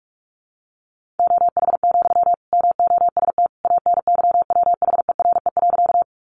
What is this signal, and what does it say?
Morse code, heard as a keyed tone, about 44 wpm, 700 Hz. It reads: OH, MOST ANYWHERE.
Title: Sentences from Top 1400 Words 45wpm